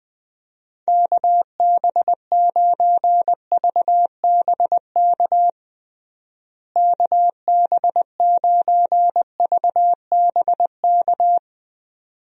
Morse code, heard as a keyed tone, about 21 wpm, 700 Hz. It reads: KB9VBK KB9VBK
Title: USA Call Signs - 6 Characters - Repeated 2x - 20wpm